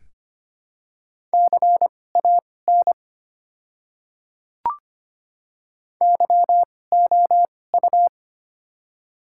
Morse code, heard as a keyed tone, about 25 wpm, spaced as for 18 wpm, 700 Hz. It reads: CAN YOU
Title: 2x Character Spacing - Top 100 Words - 25wpm